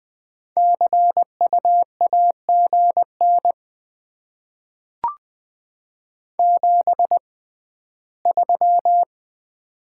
Morse code, heard as a keyed tone, about 20 wpm, spaced as for 14 wpm, 700 Hz. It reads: CU AGN 73
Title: Top 20 QSO Elements - 6x Character Spacing - 20wpm